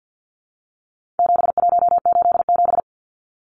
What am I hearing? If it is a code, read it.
Morse code, heard as a keyed tone, about 50 wpm, 700 Hz. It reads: 7187
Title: Numbers - 4-Digit 50wpm